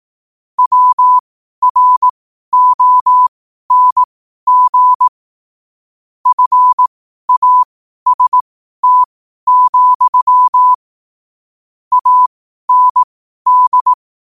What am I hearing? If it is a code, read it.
Morse code, heard as a keyed tone, about 18 wpm, spaced as for 12 wpm, 1000 Hz. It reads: WRONG FAST, AND
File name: qotdfiles / 12wpm